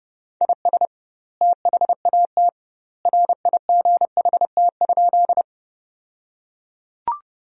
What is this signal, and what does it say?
Morse code, heard as a keyed tone, about 30 wpm, 700 Hz. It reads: IS THAT RIGHT?
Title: 2x Word-Spacing - Sentences from Top 200 Words - 30wpm